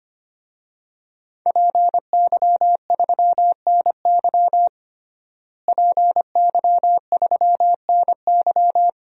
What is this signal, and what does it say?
Morse code, heard as a keyed tone, about 25 wpm, 700 Hz. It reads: PY3NY PY3NY
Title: DXCC-100 Call Signs - 5 Characters - Repeated 2x - 25wpm